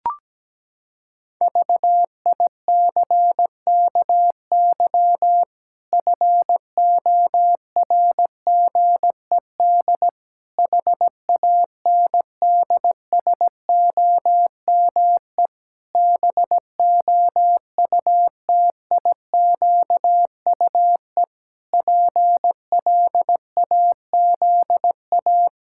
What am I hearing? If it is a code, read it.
Morse code, heard as a keyed tone, about 17 wpm, 700 Hz. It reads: VICKY FORGED HANDSOME BOUTIQUE PLAZA
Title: Pangram Challenge v2 - 17wpm